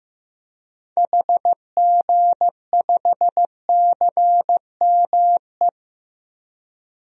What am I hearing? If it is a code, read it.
Morse code, heard as a keyed tone, about 15 wpm, 700 Hz. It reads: HG5CME
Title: DXCC-100 Call Signs - 6 Characters - Repeated 3x - 15wpm